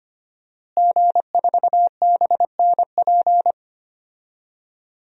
Morse code, heard as a keyed tone, about 25 wpm, 700 Hz. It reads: G4BNP
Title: DXCC-100 Call Signs - 5 Characters - Repeated 3x - 25wpm